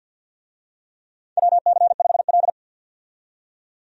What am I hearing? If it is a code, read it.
Morse code, heard as a keyed tone, about 52 wpm, 700 Hz. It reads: WC5L
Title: USA Call Signs - 4 Characters - 50wpm